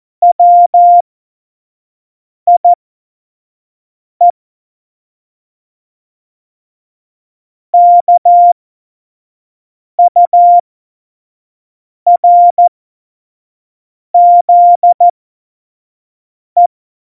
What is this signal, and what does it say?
Morse code, heard as a keyed tone, about 14 wpm, spaced as for 5 wpm, 700 Hz. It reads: WIE KURZE
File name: Morsekurs / Witze / Witz 16. April 2019, 25 F70